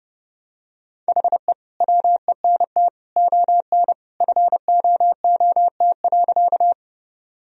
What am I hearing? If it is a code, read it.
Morse code, heard as a keyed tone, about 32 wpm, 700 Hz. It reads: HE WENT ON FOOT.
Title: Sentences from Top 1000 Words - Review Part 3 - 30wpm